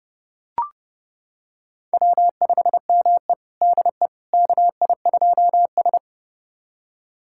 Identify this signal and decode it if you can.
Morse code, heard as a keyed tone, about 30 wpm, 700 Hz. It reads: W5ME DE KI2S
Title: USA Callsign DE Callsign - 4 Characters - 30wpm